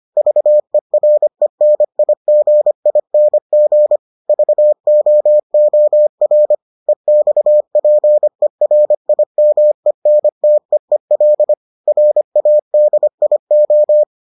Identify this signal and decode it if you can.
Morse code, heard as a keyed tone, about 25 wpm, 600 Hz. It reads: VERENIGING VOOR EXPERIMENTEEL RADIO